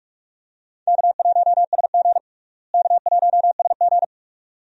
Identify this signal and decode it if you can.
Morse code, heard as a keyed tone, about 42 wpm, 700 Hz. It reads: K1SG K1SG
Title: USA Call Signs - 4 Characters - Repeated 2x - 45wpm